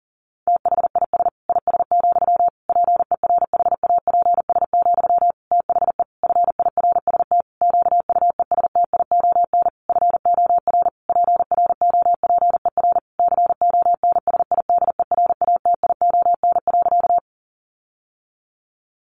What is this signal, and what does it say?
Morse code, heard as a keyed tone, about 40 wpm, 700 Hz. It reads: THIS IS, PERHAPS, THE FIRST QUESTION FOR PROPER CONSIDERATION.
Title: Sentences from Top 1500 Words 40wpm